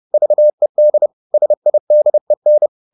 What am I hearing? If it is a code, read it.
Morse code, heard as a keyed tone, about 30 wpm, 600 Hz. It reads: VED SIDEN
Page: Ferie 30 wpm